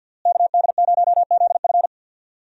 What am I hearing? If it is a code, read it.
Morse code, heard as a keyed tone, about 46 wpm, 700 Hz. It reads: KD0ZF
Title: USA Call Signs - 5 Characters - Repeated 3x - 50wpm